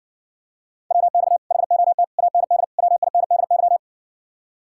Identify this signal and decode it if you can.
Morse code, heard as a keyed tone, about 60 wpm, 700 Hz. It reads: WX HOT AND WINDY